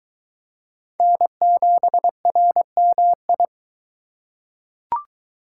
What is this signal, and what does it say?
Morse code, heard as a keyed tone, about 22 wpm, 700 Hz. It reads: N7RMI